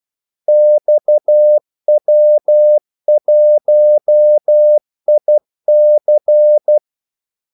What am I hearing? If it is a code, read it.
Morse code, heard as a keyed tone, about 12 wpm, 600 Hz. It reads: XW1IC
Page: 2 ekstra ordmellemrum mellem kaldesignalerne